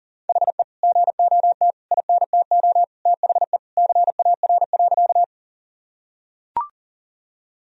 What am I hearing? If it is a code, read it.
Morse code, heard as a keyed tone, about 40 wpm, 700 Hz. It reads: HE GOT INTO THE CAR.
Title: Sentences from Top 1000 Words - Review Part 1 - 40wpm